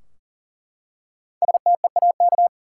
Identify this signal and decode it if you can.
Morse code, heard as a keyed tone, about 40 wpm, 700 Hz. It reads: STEAK
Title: Five Letter Words - 40wpm